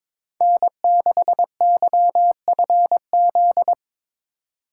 Morse code, heard as a keyed tone, about 22 wpm, 700 Hz. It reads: N6YFZ
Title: USA Call Signs - 5 Characters - 2x Character Spacing - 22wpm